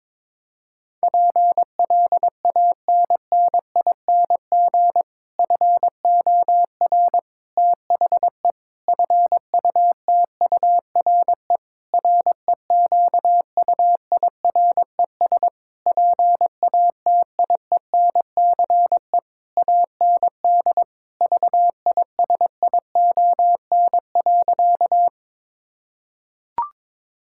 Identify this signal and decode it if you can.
Morse code, heard as a keyed tone, about 22 wpm, 700 Hz. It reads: PLANNING FOR THE FUTURE REQUIRES PATIENCE AND VISION.
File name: Difficult Sentences with Hints - 22wpm